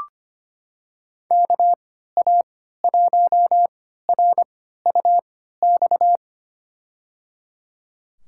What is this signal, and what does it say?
Morse code, heard as a keyed tone, about 25 wpm, spaced as for 14 wpm, 700 Hz. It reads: KA1RUX
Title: USA Call Signs - 6 Characters - 3x Character Spacing - 25wpm